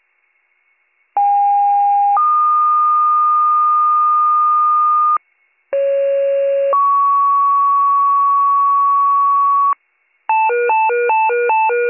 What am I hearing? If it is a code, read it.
Two-tone page, heard as a one-second tone, then a three-second tone, about 810 then 1220 Hz, about 550 then 1120 Hz. It is heard from 1.2 to 9.7 s.